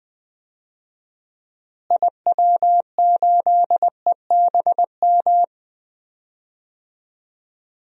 Morse code, heard as a keyed tone, about 20 wpm, 700 Hz. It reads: IW8EBM
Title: DXCC-100 Call Signs - 6 Characters - Repeated 3x - 20wpm